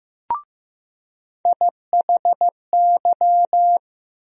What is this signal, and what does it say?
Morse code, heard as a keyed tone, about 15 wpm, 700 Hz. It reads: IHY
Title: USA Call Sign Postfixes - 3 Letters 15wpm